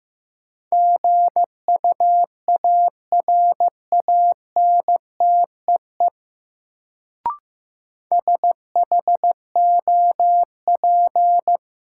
Morse code, heard as a keyed tone, about 15 wpm, 700 Hz. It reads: GUARANTEE SHOP
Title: Sentences from Top 1600 Words - New Word Drill - 15wpm